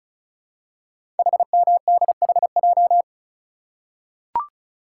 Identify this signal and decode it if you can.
Morse code, heard as a keyed tone, about 36 wpm, 700 Hz. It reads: HMDHJ
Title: Code Groups - 5 Letters - 35wpm